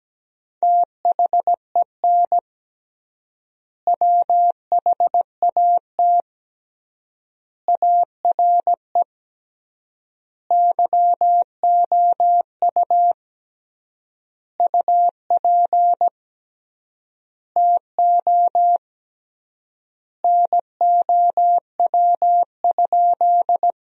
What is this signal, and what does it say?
Morse code, heard as a keyed tone, about 17 wpm, 700 Hz. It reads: THEN WHAT ARE YOU UP TO NOW?
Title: 3x Word-Spacing - Sentences from Top 100 Words - 17wpm